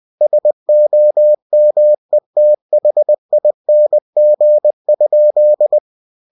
Morse code, heard as a keyed tone, about 20 wpm, 600 Hz. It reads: SOMETHING?
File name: cw-books / winnie-the-pooh / winnie-20wpm-page-0015